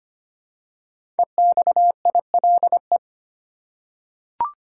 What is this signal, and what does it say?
Morse code, heard as a keyed tone, about 25 wpm, 700 Hz. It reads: EXILE
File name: 5 Letter Words - First 3 Letters Given - 25wpm